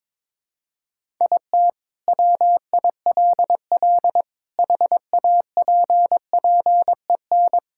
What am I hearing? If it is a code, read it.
Morse code, heard as a keyed tone, about 21 wpm, 700 Hz. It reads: IT WILL HAPPEN
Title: Sentences from Top 1000 Words - Review Part 5 - 22wpm